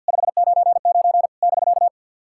Morse code, heard as a keyed tone, about 50 wpm, 690 Hz. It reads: 599 <BK>
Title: SOTA QSO Practice 50wpm